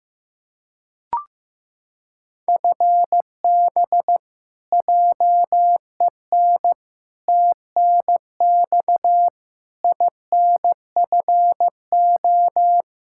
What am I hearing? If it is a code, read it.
Morse code, heard as a keyed tone, about 15 wpm, 700 Hz. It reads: FB JEN TNX INFO
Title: Practice QSO - Exchange 3 - More Info - 15wpm